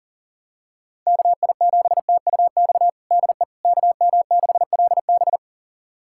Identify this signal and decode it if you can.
Morse code, heard as a keyed tone, about 37 wpm, 700 Hz. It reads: KI7TUX DE KM6LB
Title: USA Callsign DE Callsign 40wpm